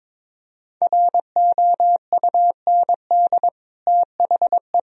Morse code, heard as a keyed tone, about 21 wpm, 700 Hz. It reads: ROUND THE
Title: Sentences from Top 1000 Words - Review Part 1 - 22wpm